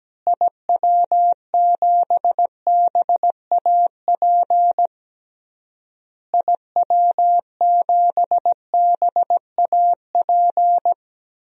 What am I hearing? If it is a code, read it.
Morse code, heard as a keyed tone, about 17 wpm, 700 Hz. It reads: IW7BAP IW7BAP
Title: DXCC-100 Call Signs - 6 Characters - Repeated 3x - 17wpm